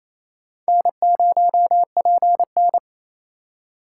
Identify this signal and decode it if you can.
Morse code, heard as a keyed tone, about 28 wpm, 700 Hz. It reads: N0PN
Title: USA Call Signs - 4 Characters - 8x Character Spacing - 28wpm